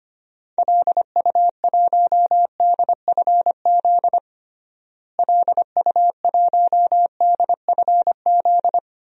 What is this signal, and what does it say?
Morse code, heard as a keyed tone, about 25 wpm, 700 Hz. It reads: LU1DFZ LU1DFZ
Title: DXCC-100 Call Signs - 6 Characters - Repeated 2x - 25wpm